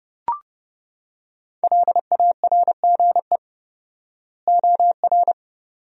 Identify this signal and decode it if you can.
Morse code, heard as a keyed tone, about 32 wpm, 700 Hz. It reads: LARGE OR
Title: 4x Word-Spacing - Sentences from Top 200 Words - 30wpm